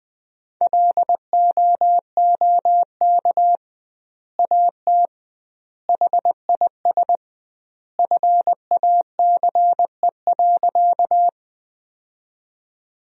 Morse code, heard as a keyed tone, about 20 wpm, 700 Hz. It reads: LOOK AT HIS FACE.
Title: 2x Word-Spacing - Sentences from Top 300 Words - 20wpm